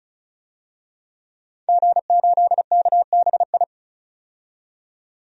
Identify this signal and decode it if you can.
Morse code, heard as a keyed tone, about 34 wpm, 700 Hz. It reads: G8KBI
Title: DXCC-100 Call Signs - 5 Characters - 35wpm